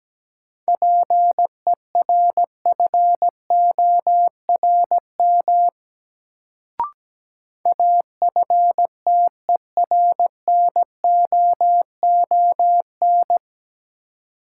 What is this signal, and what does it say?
Morse code, heard as a keyed tone, about 17 wpm, 700 Hz. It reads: PERFORM AFTERNOON
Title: Sentences from Top 1500 Words - New Word Drill Alt - 17wpm